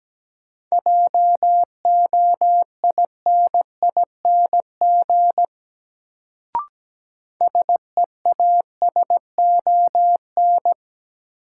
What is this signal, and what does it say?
Morse code, heard as a keyed tone, about 17 wpm, 700 Hz. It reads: JOINING SEASON
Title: Sets of 5 Words - New Word Drill - 17wpm